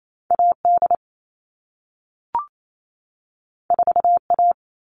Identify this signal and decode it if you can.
Morse code, heard as a keyed tone, about 28 wpm, 700 Hz. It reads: AD 4A
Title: All Cut Numbers - 2-Digit 28wpm